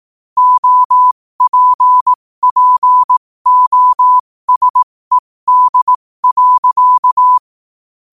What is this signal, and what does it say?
Morse code, heard as a keyed tone, about 18 wpm, spaced as for 15 wpm, 1000 Hz. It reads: OPPOSED.
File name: qotdfiles / 15wpm